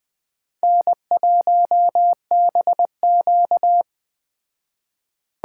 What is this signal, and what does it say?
Morse code, heard as a keyed tone, about 20 wpm, 700 Hz. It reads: N1BQ
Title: Speed Racing - USA 1x2 Call Signs 17wpm